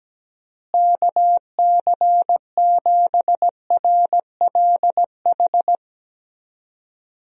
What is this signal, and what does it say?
Morse code, heard as a keyed tone, about 17 wpm, 700 Hz. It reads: KC7RLH
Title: USA Call Signs - 6 Characters - Repeated 3x - 17wpm